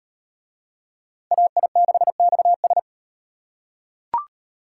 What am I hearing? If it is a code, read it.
Morse code, heard as a keyed tone, about 33 wpm, 700 Hz. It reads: AI6XS